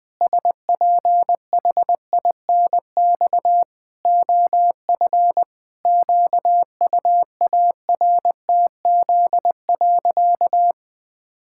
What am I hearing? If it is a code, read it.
Morse code, heard as a keyed tone, about 20 wpm, 700 Hz. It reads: SPHINX OF QUARTZ.
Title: Pangram Challenge v2 - 20wpm